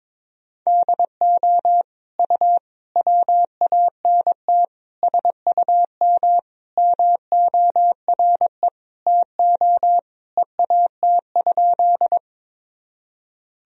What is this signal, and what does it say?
Morse code, heard as a keyed tone, about 21 wpm, 700 Hz. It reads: DO U WANT SUM MORE TO EAT?
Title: Sentences from Top 300 Words and Abbreviations 22wpm